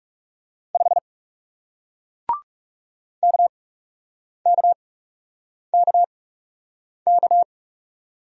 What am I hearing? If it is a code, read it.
Morse code, heard as a keyed tone, about 40 wpm, spaced as for 8 wpm, 700 Hz. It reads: 5 KKKK